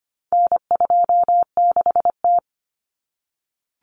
Morse code, heard as a keyed tone, about 25 wpm, 700 Hz. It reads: N26T
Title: Common Cut Numbers - 4-Digit 25wpm